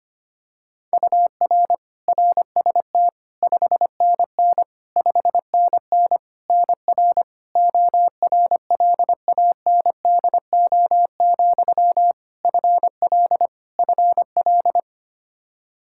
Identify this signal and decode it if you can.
Morse code, heard as a keyed tone, about 25 wpm, 700 Hz. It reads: UR RST 5NN 5NN NR ORLANDO, FL FL
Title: Practice QSO - Exchange 1 - Beginning - 25wpm